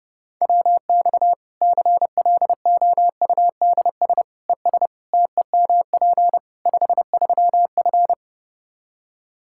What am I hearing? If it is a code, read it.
Morse code, heard as a keyed tone, about 30 wpm, 700 Hz. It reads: WX CLOUDS ES TEMP 53F
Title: Practice QSO - Exchange 2 - Info - 30wpm